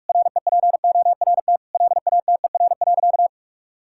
Morse code, heard as a keyed tone, about 43 wpm, 700 Hz. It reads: REPORT LATER.
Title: Sentences from Top 1000 Words - Review Part 6 - 45wpm